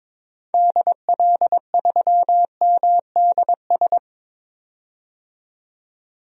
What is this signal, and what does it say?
Morse code, heard as a keyed tone, about 22 wpm, 700 Hz. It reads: DL3MDS
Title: DXCC-100 Call Signs - 6 Characters - 22wpm